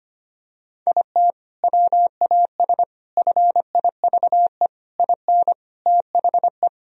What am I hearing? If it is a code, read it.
Morse code, heard as a keyed tone, about 25 wpm, 700 Hz. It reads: IT WAS FIVE IN THE
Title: Sentences from Top 1000 Words - Review Part 2 - 25wpm